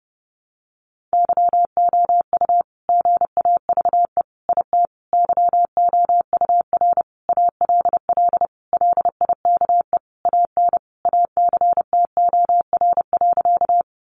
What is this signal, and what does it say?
Morse code, heard as a keyed tone, about 30 wpm, 700 Hz. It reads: YOU GAVE IT YOUR ALL LIKE AN ACTOR.
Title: Sentences from Top 1500 Words 30wpm